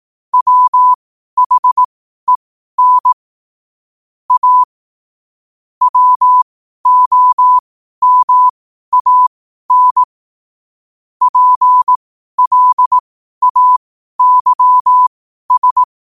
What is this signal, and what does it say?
Morse code, heard as a keyed tone, about 18 wpm, spaced as for 12 wpm, 1000 Hz. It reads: WHEN A WOMAN PLAYS
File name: qotdfiles / 12wpm